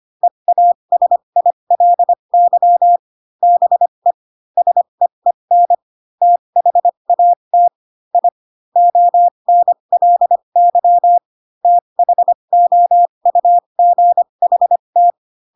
Morse code, heard as a keyed tone, about 25 wpm, spaced as for 22 wpm, 700 Hz. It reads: EASILY BE SEEN THAT I ONLY THOUGHT